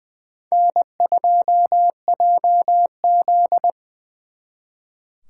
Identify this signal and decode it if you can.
Morse code, heard as a keyed tone, about 25 wpm, 700 Hz. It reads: N2JZ
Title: Speed Racing - USA 1x2 Call Signs 20wpm